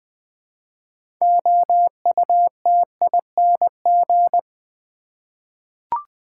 Morse code, heard as a keyed tone, about 20 wpm, 700 Hz. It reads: OUTING
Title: 6 Letter Words - First 3 Letters Given - 20wpm